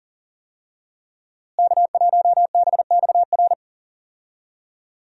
Morse code, heard as a keyed tone, about 40 wpm, 700 Hz. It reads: K1BXR
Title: USA Call Signs - 5 Characters - Repeated 3x - 40wpm